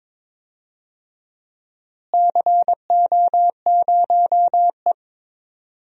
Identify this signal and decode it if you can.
Morse code, heard as a keyed tone, about 22 wpm, 700 Hz. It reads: CO0E
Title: DXCC-100 Call Signs - 4 Characters - Repeated 3x - 22wpm